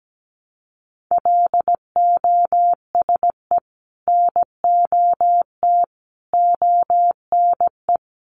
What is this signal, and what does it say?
Morse code, heard as a keyed tone, about 17 wpm, 700 Hz. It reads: LOSE NOT ONE
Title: Sentences from Top 1100 Words 17wpm